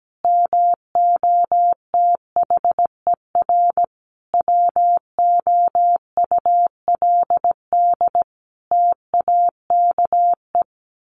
Morse code, heard as a keyed tone, about 17 wpm, 700 Hz. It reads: MOTHER WOULD TAKE
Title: Sentences from Top 700 Words 17wpm